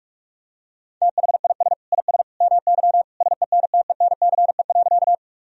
Morse code, heard as a keyed tone, about 46 wpm, 700 Hz. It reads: THIS IS MY SENTENCE.
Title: Sentences from Top 1000 Words - Review Part 1 - 45wpm